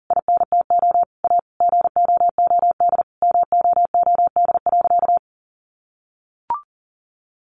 Morse code, heard as a keyed tone, about 40 wpm, 700 Hz. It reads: INTO A GOOD MOOD.
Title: Sentences from Top 1100 Words 40wpm